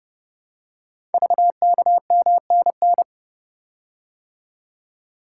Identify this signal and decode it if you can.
Morse code, heard as a keyed tone, about 30 wpm, 700 Hz. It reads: VKMNN
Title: Code Groups - 5 Letters - 30wpm